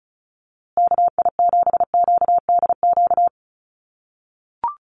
Morse code, heard as a keyed tone, about 34 wpm, 700 Hz. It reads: KI7QDQ
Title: USA Call Signs Repeated 2x 35wpm